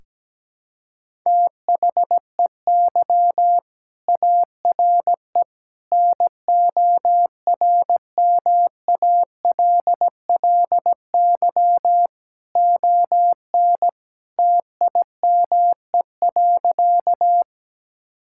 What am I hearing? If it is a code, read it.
Morse code, heard as a keyed tone, about 17 wpm, 700 Hz. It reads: THEY ARE NORMALLY ON TIME.
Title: Sentences from Top 1000 Words - Review Part 6 - 17wpm